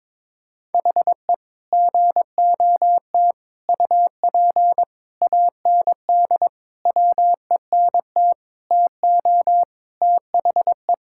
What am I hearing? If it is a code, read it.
Morse code, heard as a keyed tone, about 22 wpm, 700 Hz. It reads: HE GOT UP AND WENT TO THE
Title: Sentences from Top 1600 Words 22wpm